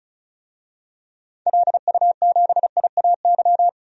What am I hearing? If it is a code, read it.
Morse code, heard as a keyed tone, about 36 wpm, 700 Hz. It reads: LU7IAY
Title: DXCC-100 Call Signs - 6 Characters - 35wpm